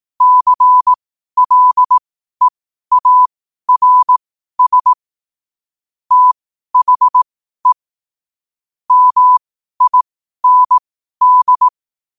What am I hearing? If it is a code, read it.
Morse code, heard as a keyed tone, about 18 wpm, spaced as for 12 wpm, 1000 Hz. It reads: CLEARS THE MIND
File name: qotdfiles / 12wpm